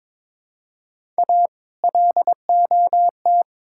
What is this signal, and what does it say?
Morse code, heard as a keyed tone, about 22 wpm, 700 Hz. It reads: A LOT
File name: Build A Phrase - 2-Word Phrases - 22wpm